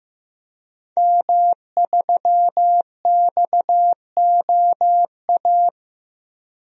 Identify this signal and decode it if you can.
Morse code, heard as a keyed tone, about 15 wpm, 700 Hz. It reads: M3XOA
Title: DXCC-100 Call Signs - 5 Characters - 15wpm